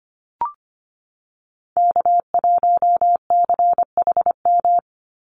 Morse code, heard as a keyed tone, about 25 wpm, 700 Hz. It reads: K1CHM
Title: Call Signs Repeated 2x - 25wpm